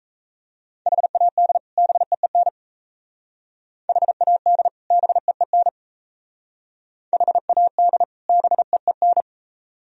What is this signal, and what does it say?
Morse code, heard as a keyed tone, about 36 wpm, 700 Hz. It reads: HAD BEEN HAD BEEN HAD BEEN